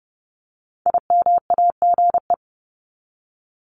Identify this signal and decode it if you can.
Morse code, heard as a keyed tone, about 30 wpm, 700 Hz. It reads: IMAGE
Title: Top 501-750 Words 30wpm